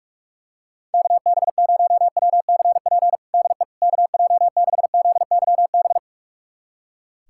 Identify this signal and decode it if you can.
Morse code, heard as a keyed tone, about 42 wpm, 700 Hz. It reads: KB0WCP DE KJ6ZYB
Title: USA Callsign DE Callsign - 6 Characters - 45wpm